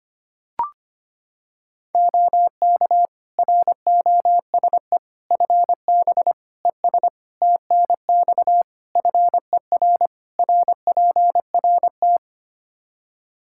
Morse code, heard as a keyed tone, about 25 wpm, 700 Hz. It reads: OK ROSE FB ES TNX FER RPRT
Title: Practice QSO - Exchange 2 - Info - 25wpm